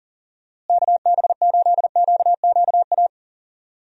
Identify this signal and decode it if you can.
Morse code, heard as a keyed tone, about 40 wpm, 700 Hz. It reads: KB8QQA
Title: USA Call Signs - 6 Characters - Repeated 2x - 40wpm